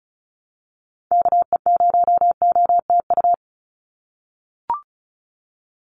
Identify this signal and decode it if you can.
Morse code, heard as a keyed tone, about 34 wpm, 700 Hz. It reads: KE0OTU
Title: USA Call Signs 35wpm